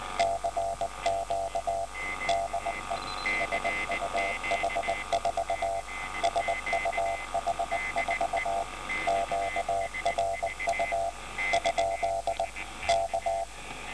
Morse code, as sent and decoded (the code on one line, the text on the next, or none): -.-. --.- / -.. / ... ..- .... ....- / ... ..- .... ....- / --.- .-. ..- / ..--.. / -.-
CQ DE SUH4 SUH4 QRU ? K